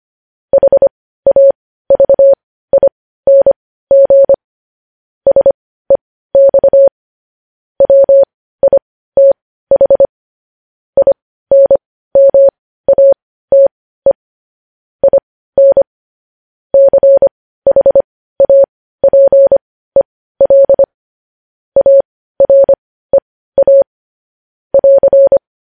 Morse code, heard as a keyed tone, about 25 wpm, spaced as for 15 wpm, 550 Hz. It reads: HAVING SEX WITH INMATE IN CHAPEL AREA <AR>